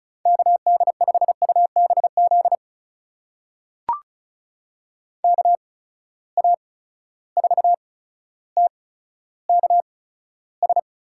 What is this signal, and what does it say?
Morse code, heard as a keyed tone, about 36 wpm, spaced as for 15 wpm, 700 Hz. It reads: KD5UBZ KA4TKS
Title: USA Call Signs - 6 Characters - 8x Character Spacing - 35wpm